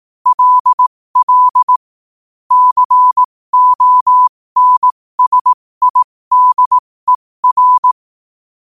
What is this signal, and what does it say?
Morse code, heard as a keyed tone, about 18 wpm, spaced as for 15 wpm, 1000 Hz. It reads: LL CONSIDER
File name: qotdfiles / 15wpm